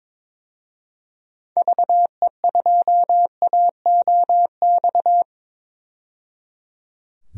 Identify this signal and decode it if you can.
Morse code, heard as a keyed tone, about 22 wpm, 700 Hz. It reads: VE2AOX
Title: DXCC-100 Call Signs - 6 Characters - 22wpm